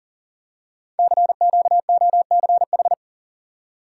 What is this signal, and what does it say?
Morse code, heard as a keyed tone, about 39 wpm, 700 Hz. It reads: CQOCH
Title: Code Groups - 5 Letter:Numbers - 40wpm